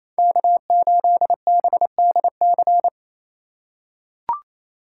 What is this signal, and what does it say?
Morse code, heard as a keyed tone, about 28 wpm, 700 Hz. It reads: K8BDC
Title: USA Call Signs - 5 Characters - 4x Character Spacing - 28wpm